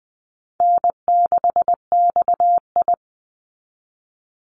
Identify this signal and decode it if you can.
Morse code, heard as a keyed tone, about 20 wpm, 700 Hz. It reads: N6XI
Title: Easy Call Signs Repeated 3x - 20wpm